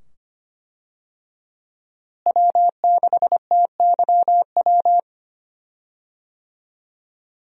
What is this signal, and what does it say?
Morse code, heard as a keyed tone, about 25 wpm, 700 Hz. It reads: W6TYW
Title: USA Call Signs - 5 Characters - Repeated 2x - 25wpm